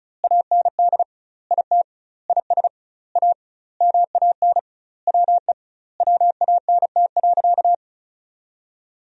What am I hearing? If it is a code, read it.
Morse code, heard as a keyed tone, about 36 wpm, 700 Hz. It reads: AND IT IS A MAN WE WANT.
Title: 2x Word-Spacing - Sentences from Top 200 Words - 35wpm